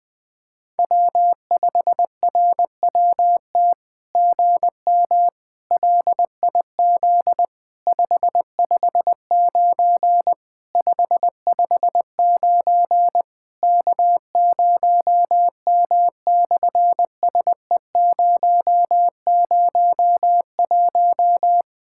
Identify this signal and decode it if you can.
Morse code, heard as a keyed tone, about 20 wpm, 700 Hz. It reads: W5RWT GM LIZ 559 559 K0M/SE001
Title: SOTA QSO Practice 20wpm